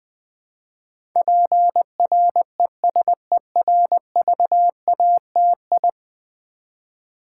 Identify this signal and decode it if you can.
Morse code, heard as a keyed tone, about 20 wpm, 700 Hz. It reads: PRESERVATI
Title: Build A Word - 5 Letters At A Time - 12-Letter Words - 20wpm